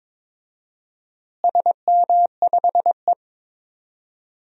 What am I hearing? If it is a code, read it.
Morse code, heard as a keyed tone, about 22 wpm, 700 Hz. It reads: SM5E
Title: DXCC-100 Call Signs - 4 Characters - Repeated 2x - 22wpm